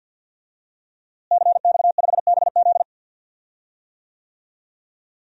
Morse code, heard as a keyed tone, about 52 wpm, 700 Hz. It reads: CX5BZ
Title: DXCC-100 Call Signs - 5 Characters - Repeated 2x - 50wpm